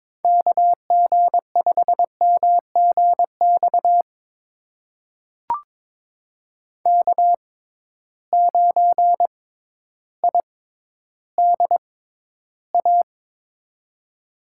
Morse code, heard as a keyed tone, about 22 wpm, spaced as for 12 wpm, 700 Hz. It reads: KG5MGX K9IDA
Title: USA Call Signs - 6x Character Spacing - 22wpm